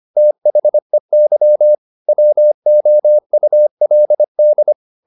Morse code, heard as a keyed tone, about 25 wpm, 600 Hz. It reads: THEY WOULD
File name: the old man and the sea part 1 25wpm